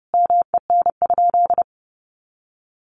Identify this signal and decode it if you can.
Morse code, heard as a keyed tone, about 30 wpm, 700 Hz. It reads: MEN?
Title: Sentences from Top 200 Words 30wpm